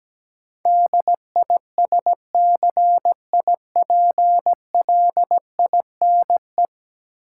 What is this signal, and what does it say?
Morse code, heard as a keyed tone, about 17 wpm, 700 Hz. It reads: DISCIPLINE
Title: Sentences from Top 900 Words - New Word Drill - 17wpm